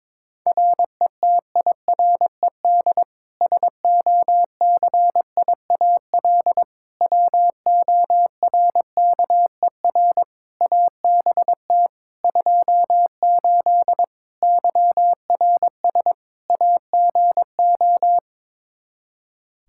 Morse code, heard as a keyed tone, about 22 wpm, 700 Hz. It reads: RETIRED SOCIAL WORKER ABT 28 YRS AGO
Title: K1USN SST QSO Practice 22wpm